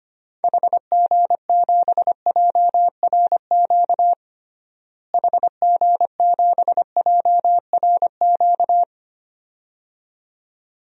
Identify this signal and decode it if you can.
Morse code, heard as a keyed tone, about 25 wpm, 700 Hz. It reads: HG7JRQ HG7JRQ
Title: DXCC-100 Call Signs - 6 Characters - Repeated 2x - 25wpm